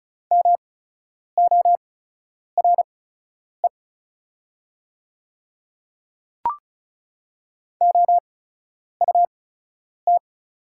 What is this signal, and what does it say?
Morse code, heard as a keyed tone, about 36 wpm, spaced as for 10 wpm, 700 Hz. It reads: MORE OUT
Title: 8x Character Spacing - Top 100 Words - 35wpm